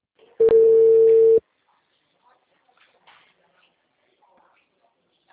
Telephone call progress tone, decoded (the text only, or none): ringback tone